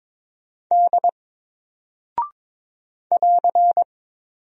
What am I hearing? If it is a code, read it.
Morse code, heard as a keyed tone, about 22 wpm, 700 Hz. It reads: D <AR>
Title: Single Character (Letter:Number:Punctuation:Prosign) Alt 22wpm